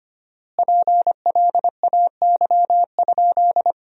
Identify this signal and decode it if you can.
Morse code, heard as a keyed tone, about 25 wpm, 700 Hz. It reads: PLAY?
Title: 4x Word-Spacing - Sentences from Top 300 Words - 25wpm